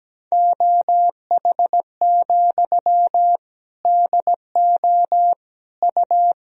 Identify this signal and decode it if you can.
Morse code, heard as a keyed tone, about 17 wpm, 700 Hz. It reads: OH, DO U
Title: Sentences from Top 500 Words and Abbreviations 17wpm